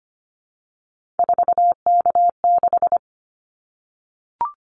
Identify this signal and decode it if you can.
Morse code, heard as a keyed tone, about 25 wpm, 700 Hz. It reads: 4K6
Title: Callsign Prefixes for Top 100 DXCC - 25wpm